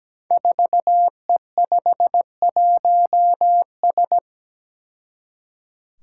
Morse code, heard as a keyed tone, about 17 wpm, 700 Hz. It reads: 4E51S
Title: Five Challenging Letter-Number Combinations 17wpm